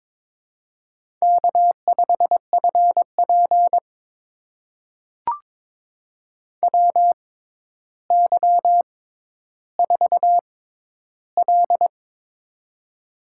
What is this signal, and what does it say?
Morse code, heard as a keyed tone, about 22 wpm, spaced as for 11 wpm, 700 Hz. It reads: K5FP WY4L
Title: USA Call Signs - 4 Characters - 6x Character Spacing - 22wpm